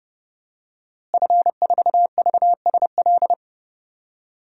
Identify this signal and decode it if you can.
Morse code, heard as a keyed tone, about 30 wpm, 700 Hz. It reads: F4VSL
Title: DXCC-100 Call Signs - 5 Characters - Repeated 2x - 30wpm